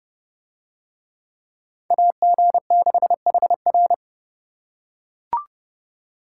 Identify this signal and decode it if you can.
Morse code, heard as a keyed tone, about 27 wpm, 700 Hz. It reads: AG6HR